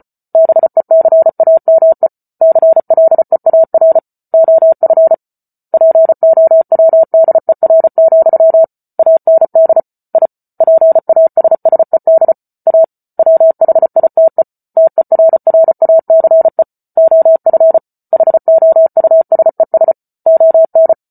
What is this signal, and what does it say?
Morse code, heard as a keyed tone, about 34 wpm, 650 Hz. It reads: BECAME CLEAR OF POWDER, AND I PASSED A WHITE TERRACE OF HOUSES ON